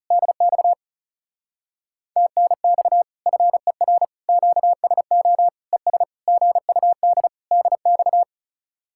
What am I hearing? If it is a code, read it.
Morse code, heard as a keyed tone, about 37 wpm, 700 Hz. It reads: DX TNX FER QSO ES GUD DX
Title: Speed-Racing Common QSO Phrases - 30wpm